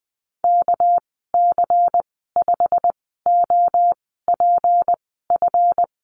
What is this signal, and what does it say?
Morse code, heard as a keyed tone, about 20 wpm, spaced as for 14 wpm, 700 Hz. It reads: KC5OPF
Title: 207 Lesson 33g - Letter j - Call Signs 20wpm